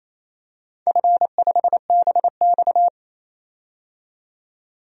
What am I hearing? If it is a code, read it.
Morse code, heard as a keyed tone, about 28 wpm, 700 Hz. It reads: F5BX
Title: DXCC-100 Call Signs - 4 Characters - 28wpm